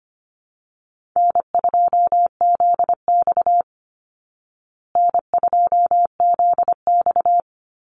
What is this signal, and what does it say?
Morse code, heard as a keyed tone, about 25 wpm, 700 Hz. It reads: N2ZX N2ZX
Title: Easy Call Signs Repeated 2x - 25wpm